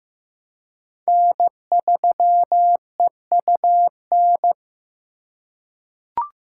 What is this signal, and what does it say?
Morse code, heard as a keyed tone, about 15 wpm, 700 Hz. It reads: N3EUN
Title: USA Call Signs - 5 Characters - 4x Character Spacing - 15wpm